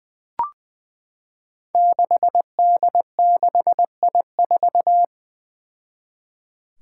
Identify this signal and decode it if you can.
Morse code, heard as a keyed tone, about 20 wpm, 700 Hz. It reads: 6D6I4
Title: Five Challenging Letter-Number Combinations 20wpm